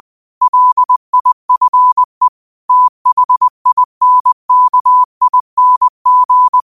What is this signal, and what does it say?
Morse code, heard as a keyed tone, about 20 wpm, 1000 Hz. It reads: LIFE THINKING
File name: qotdfiles / 20wpm